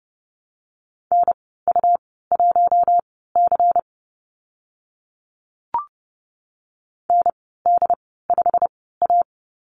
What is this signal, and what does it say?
Morse code, heard as a keyed tone, about 32 wpm, spaced as for 17 wpm, 700 Hz. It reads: NU1C ND5A
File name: Lesson 19g - Letter w - Call Signs 30wpm